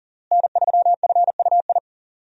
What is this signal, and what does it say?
Morse code, heard as a keyed tone, about 40 wpm, 700 Hz. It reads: N3FUI
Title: USA Call Signs - 5 Characters - Repeated 2x - 40wpm